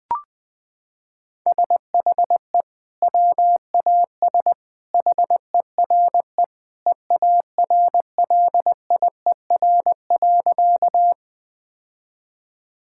Morse code, heard as a keyed tone, about 20 wpm, 700 Hz. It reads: SHE WAS HERE EARLIER.
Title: Sentences with 2-Word Phrases - 20wpm